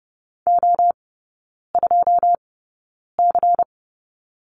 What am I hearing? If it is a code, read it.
Morse code, heard as a keyed tone, about 30 wpm, spaced as for 9 wpm, 700 Hz. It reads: O2C
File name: Lesson 16e - Letter c - Three Characters 30wpm